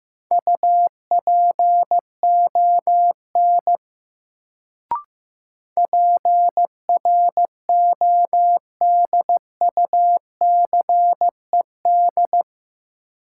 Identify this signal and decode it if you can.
Morse code, heard as a keyed tone, about 15 wpm, 700 Hz. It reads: UPON PRODUCED
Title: Sentences from Top 1000 Words - Review Part 6 - New Word Drill - 15wpm